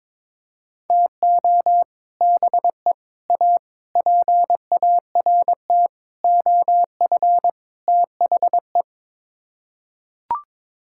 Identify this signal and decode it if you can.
Morse code, heard as a keyed tone, about 22 wpm, 700 Hz. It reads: TO BE A PART OF THE
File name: practice / 22wpm / Sets of 6 Words - 22wpm